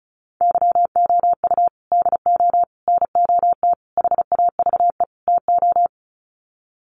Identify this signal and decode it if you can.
Morse code, heard as a keyed tone, about 34 wpm, 700 Hz. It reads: YOU DO NOT HAVE TO
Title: Sets of 5 Words 35wpm